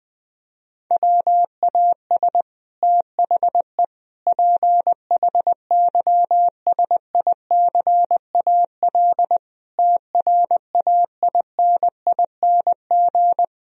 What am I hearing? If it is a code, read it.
Morse code, heard as a keyed tone, about 20 wpm, 700 Hz. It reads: WAS THE PHYSICAL TRAINING
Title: Sentences from Top 1000 Words - Review Part 4 - 20wpm